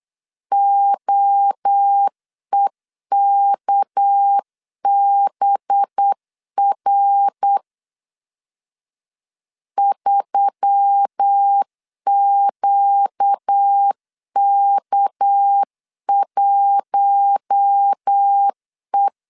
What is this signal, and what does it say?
Morse code, heard as a keyed tone, about 8 wpm, 800 Hz. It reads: OEKBR 3QK1E
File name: CW / DET / 1 6 1500C S C 8wpm 5219